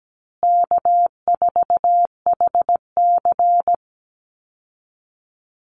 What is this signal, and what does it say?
Morse code, heard as a keyed tone, about 17 wpm, 700 Hz. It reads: K4HC
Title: Easy Call Signs Repeated 3x - 17wpm